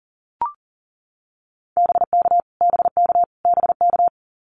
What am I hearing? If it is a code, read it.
Morse code, heard as a keyed tone, about 40 wpm, 700 Hz. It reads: BK BK BK
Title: Abbreviations Repeated 3x - 40wpm